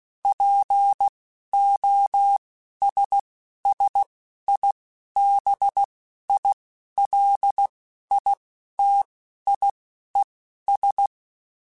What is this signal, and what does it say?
Morse code, heard as a keyed tone, about 16 wpm, spaced as for 11 wpm, 800 Hz. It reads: POSSIBILITIES
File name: Technical Subjects / Technical Subjects-mp3s / OR Gate @10